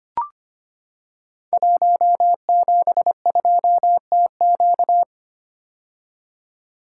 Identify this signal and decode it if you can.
Morse code, heard as a keyed tone, about 25 wpm, 700 Hz. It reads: 172TQ
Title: Code Groups - 5 Letter:Numbers - 25wpm